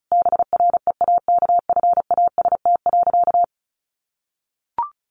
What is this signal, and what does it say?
Morse code, heard as a keyed tone, about 34 wpm, 700 Hz. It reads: BREAKFAST.
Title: Easy Phrases 35wpm